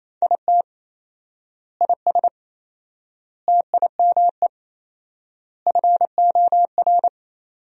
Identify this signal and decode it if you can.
Morse code, heard as a keyed tone, about 28 wpm, 700 Hz. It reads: IT IS TIME FOR
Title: 4x Word-Spacing - Sentences from Top 500 Words - 28wpm